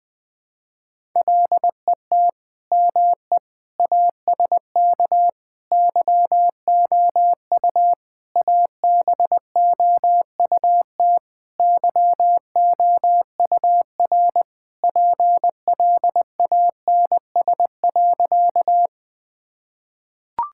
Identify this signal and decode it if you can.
Morse code, heard as a keyed tone, about 20 wpm, 700 Hz. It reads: LET ME ASK YOU ABOUT YOUR PLANS.
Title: Sentences with 5-Word Phrases - 20wpm